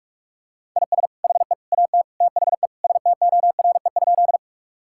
Morse code, heard as a keyed tone, about 46 wpm, 700 Hz. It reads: IS HE AT THE STORE?
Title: Sentences from Top 1000 Words - Review Part 5 - 45wpm